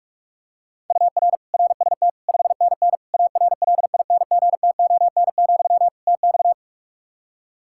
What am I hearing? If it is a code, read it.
Morse code, heard as a keyed tone, about 46 wpm, 700 Hz. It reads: UR RST 5NN ARLINGTON, TX
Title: Practice QSO - Exchange 1 - Beginning - 45wpm